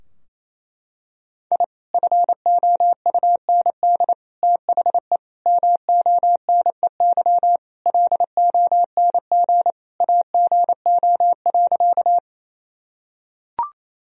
Morse code, heard as a keyed tone, about 28 wpm, 700 Hz. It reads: I FOUND THE MONEY LONG AGO.
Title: Sentences from Top 1000 Words - Review Part 3 - 28wpm